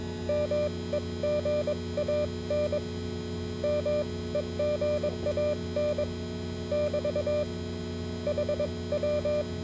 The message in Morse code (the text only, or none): MEGAN MEGAN <BT> HW